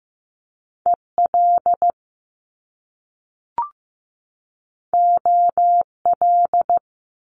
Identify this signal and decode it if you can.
Morse code, heard as a keyed tone, about 15 wpm, 700 Hz. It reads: EL OL
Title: Top Two Letter Combinations in Words 15wpm